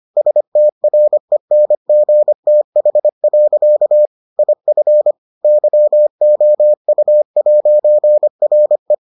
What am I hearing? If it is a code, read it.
Morse code, heard as a keyed tone, about 25 wpm, 600 Hz. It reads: STRENGTH. IF YOU'RE